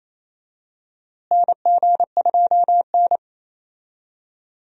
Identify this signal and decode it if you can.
Morse code, heard as a keyed tone, about 27 wpm, 700 Hz. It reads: NG2N
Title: USA Call Signs - 4 Characters - Repeated 2x - 28wpm